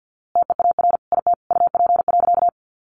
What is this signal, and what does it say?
Morse code, heard as a keyed tone, about 50 wpm, 700 Hz. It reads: TEAR IT UP.
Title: A Sound Of Thunder 50wpm